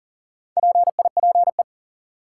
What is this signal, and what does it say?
Morse code, heard as a keyed tone, about 40 wpm, 700 Hz. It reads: PIPE
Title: Sentences from Top 1500 Words - New Word Drill Alt - 40wpm